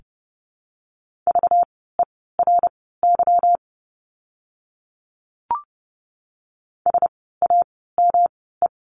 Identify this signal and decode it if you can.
Morse code, heard as a keyed tone, about 30 wpm, spaced as for 17 wpm, 700 Hz. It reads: VERY SAME
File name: course / 30:10wpm / 237 Lesson 38f - Letter q - Words 30wpm